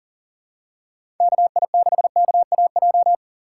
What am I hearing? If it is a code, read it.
Morse code, heard as a keyed tone, about 40 wpm, 700 Hz. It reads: KI6KAJ
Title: USA Call Signs - 6 Characters - 6x Character Spacing - 40wpm